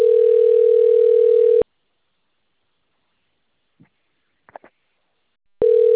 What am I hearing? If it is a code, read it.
Ringback tone (North America), 440 + 480 Hz.
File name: out- - - - - .5733